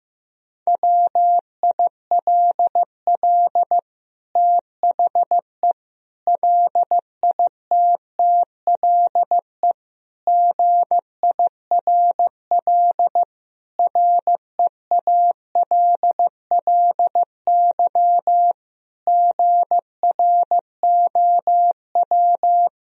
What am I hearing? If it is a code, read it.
Morse code, heard as a keyed tone, about 15 wpm, 700 Hz. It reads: WILL THE LITTLE GIRL REALLY GROW
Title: Sentences from Top 1000 Words - Review Part 1 - 15wpm